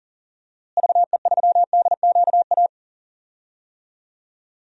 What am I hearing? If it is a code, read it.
Morse code, heard as a keyed tone, about 40 wpm, 700 Hz. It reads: VE3DQA
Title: DXCC-100 Call Signs - 6 Characters - Repeated 3x - 40wpm